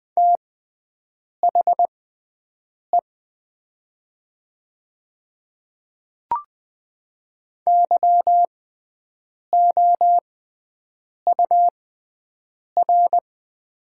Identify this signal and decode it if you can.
Morse code, heard as a keyed tone, about 20 wpm, spaced as for 7 wpm, 700 Hz. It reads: THE YOUR
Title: 6x Character Spacing - Top 100 Words - 20wpm